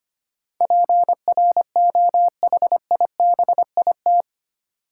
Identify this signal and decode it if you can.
Morse code, heard as a keyed tone, about 25 wpm, 700 Hz. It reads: PROHIBIT
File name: Eight Letter Words - 25wpm